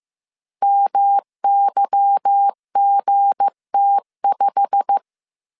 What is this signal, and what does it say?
Morse code, heard as a keyed tone, about 15 wpm, 800 Hz. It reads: MYGT5
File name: CW / DET / 1 9 1500C S A 14wpm 3222